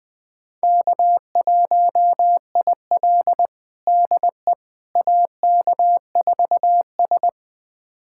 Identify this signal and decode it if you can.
Morse code, heard as a keyed tone, about 20 wpm, 700 Hz. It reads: K1IL DE AK4S
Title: USA Callsign DE Callsign - 4 Characters - 20wpm